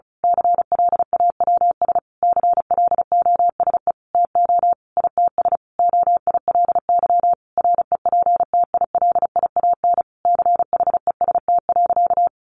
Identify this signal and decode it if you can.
Morse code, heard as a keyed tone, about 34 wpm, 700 Hz. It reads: CLAWS CLOSE TO ITS OILY REPTILIAN CHEST.
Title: A Sound Of Thunder 35wpm